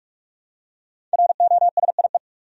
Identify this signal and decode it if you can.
Morse code, heard as a keyed tone, about 45 wpm, 700 Hz. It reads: ROSIE
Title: USA Common Names - Max 5 Letters - 45wpm